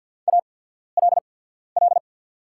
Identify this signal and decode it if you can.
Morse code, heard as a keyed tone, about 48 wpm, spaced as for 14 wpm, 690 Hz. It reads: ALL
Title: 8x Character Spacing - Top 100 Words - 50wpm